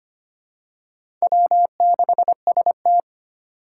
Morse code, heard as a keyed tone, about 25 wpm, 700 Hz. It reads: W6ST
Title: USA Call Signs - 2x Character Spacing - 25wpm